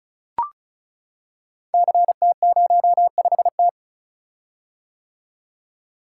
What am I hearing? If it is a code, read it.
Morse code, heard as a keyed tone, about 36 wpm, 690 Hz. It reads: CT05T
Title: Code Groups - 5 Letter:Numbers - 35wpm